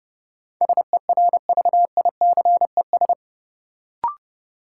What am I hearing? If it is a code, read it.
Morse code, heard as a keyed tone, about 30 wpm, 700 Hz. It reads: SERVICES
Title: Sets of 5 Words - New Word Drill - 30wpm